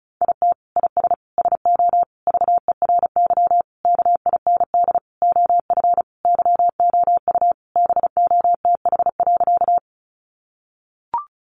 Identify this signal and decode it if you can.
Morse code, heard as a keyed tone, about 34 wpm, 700 Hz. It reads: IT IS SO VERY KIND OF YOU BOTH.
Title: Sentences from Top 300 Words 35wpm